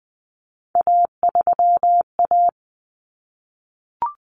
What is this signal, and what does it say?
Morse code, heard as a keyed tone, about 20 wpm, 700 Hz. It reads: A3A
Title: Common Cut Numbers - 3-Digit 20wpm